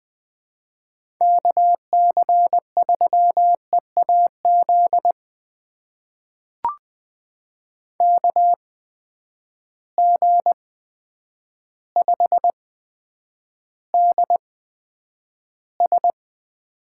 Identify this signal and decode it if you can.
Morse code, heard as a keyed tone, about 20 wpm, spaced as for 9 wpm, 700 Hz. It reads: KC3EAZ KG5DS
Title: USA Call Signs - 6 Characters - 8x Character Spacing - 20wpm